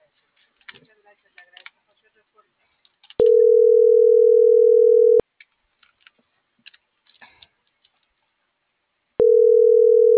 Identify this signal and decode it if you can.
Ringback tone (North America), 440 + 480 Hz.
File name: CENTRAL NMP / out melanie.diaznmp - .39787